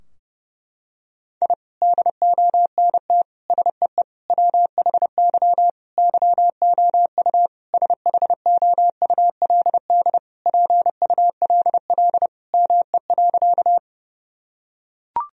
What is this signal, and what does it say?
Morse code, heard as a keyed tone, about 30 wpm, 700 Hz. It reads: I DONT SEE WHY YOU SHOULD PULL ME.
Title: Sentences from Top 1000 Words - Review Part 2 - 30wpm